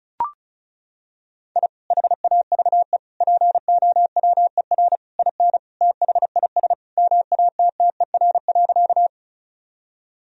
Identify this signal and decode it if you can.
Morse code, heard as a keyed tone, about 36 wpm, 700 Hz. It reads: I HAVE POWER IN THIS MATTER.
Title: Sentences from Top 1000 Words - Review Part 2 - 35wpm